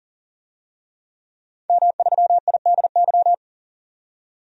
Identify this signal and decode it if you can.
Morse code, heard as a keyed tone, about 40 wpm, 700 Hz. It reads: M3IDY
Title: DXCC-100 Call Signs - 5 Characters - 40wpm